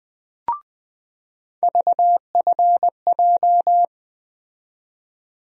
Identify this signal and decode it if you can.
Morse code, heard as a keyed tone, about 20 wpm, 700 Hz. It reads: VFJ
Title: USA Call Sign Postfixes - 3 Letters 20wpm